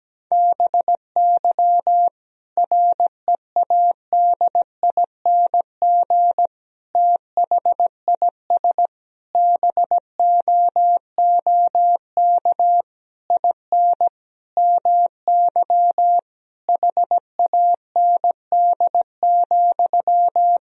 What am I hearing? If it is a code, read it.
Morse code, heard as a keyed tone, about 17 wpm, 700 Hz. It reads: BY READING THIS BOOK IN MY HAND,
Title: Sentences from Top 1000 Words - Review Part 3 - 17wpm